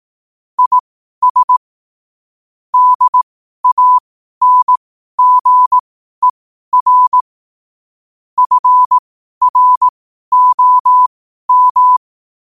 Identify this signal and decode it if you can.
Morse code, heard as a keyed tone, about 18 wpm, spaced as for 12 wpm, 1000 Hz. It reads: IS DANGER FROM
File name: qotdfiles / 12wpm